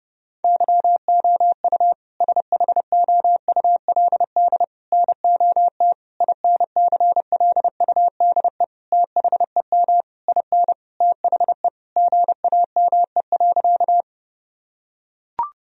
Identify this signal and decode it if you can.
Morse code, heard as a keyed tone, about 32 wpm, 700 Hz. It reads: YOU SHOULD NOT INCLUDE THEM IN THE GAME.
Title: Sentences from Top 1000 Words - Review Part 3 - 30wpm